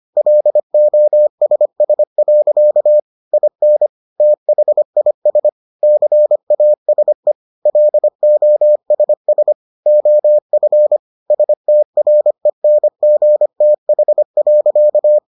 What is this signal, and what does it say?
Morse code, heard as a keyed tone, about 25 wpm, 600 Hz. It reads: LOSS. IN THIS CASE LOSS OF STRENGTH.